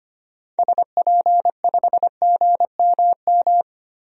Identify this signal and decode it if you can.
Morse code, heard as a keyed tone, about 25 wpm, 700 Hz. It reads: SP5GMM
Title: DXCC-100 Call Signs - 6 Characters - Repeated 2x - 25wpm